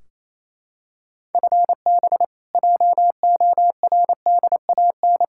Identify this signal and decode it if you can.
Morse code, heard as a keyed tone, about 28 wpm, 700 Hz. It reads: FB JORDAN
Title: Practice QSO - Exchange 3 - More Info - 28wpm